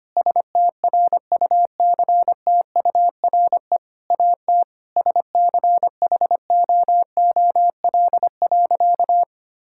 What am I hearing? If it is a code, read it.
Morse code, heard as a keyed tone, about 25 wpm, 700 Hz. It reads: STRUCTURE AT SCHOOL.
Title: Sentences from Top 1000 Words - Review Part 6 - 25wpm